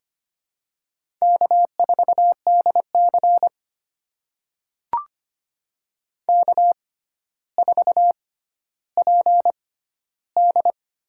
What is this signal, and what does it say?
Morse code, heard as a keyed tone, about 25 wpm, spaced as for 13 wpm, 700 Hz. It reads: K4DC K4PD
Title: USA Call Signs - 4 Characters - 6x Character Spacing - 25wpm